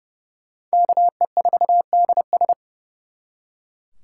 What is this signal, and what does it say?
Morse code, heard as a keyed tone, about 34 wpm, 700 Hz. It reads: KE4DS
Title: Speed Racing - USA 2x2 Call Signs 30wpm